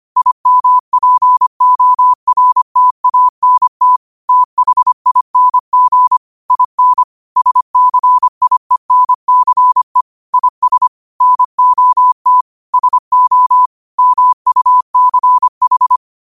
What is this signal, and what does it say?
Morse code, heard as a keyed tone, about 25 wpm, 1000 Hz. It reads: IMPORTANT THING IN SCIENCE IS NOT SO MUCH
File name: 25wpm